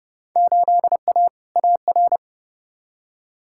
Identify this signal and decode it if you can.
Morse code, heard as a keyed tone, about 30 wpm, 700 Hz. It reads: 8A AR
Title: Field Day Exchanges - 30wpm